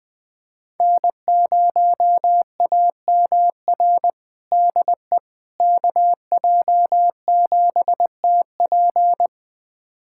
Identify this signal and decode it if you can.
Morse code, heard as a keyed tone, about 20 wpm, 700 Hz. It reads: N0AMR DE KJ7TP
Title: USA Callsign DE Callsign - 5 Characters - 20wpm